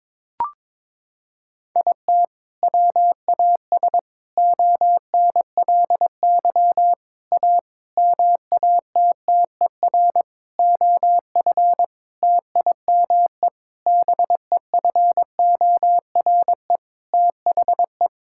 Morse code, heard as a keyed tone, about 22 wpm, 700 Hz. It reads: IT WAS ONLY A MATTER OF TIME BEFORE THE
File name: Sets of 9 Words - 22wpm